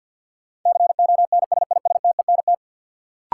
Morse code, heard as a keyed tone, about 47 wpm, 700 Hz. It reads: CONSISTENT
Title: Sentences from Top 1500 Words - New Word Drill - 50wpm